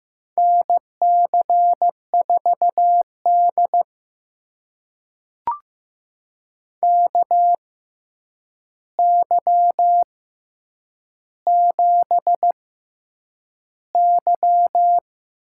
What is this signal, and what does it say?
Morse code, heard as a keyed tone, about 15 wpm, spaced as for 8 wpm, 700 Hz. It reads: NC4D KY7Y
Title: USA Call Signs - 4 Characters - 6x Character Spacing - 15wpm